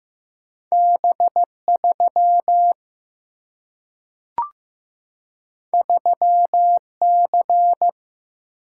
Morse code, heard as a keyed tone, about 15 wpm, 700 Hz. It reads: B3 3C
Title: Single-Letter Single-Number 15wpm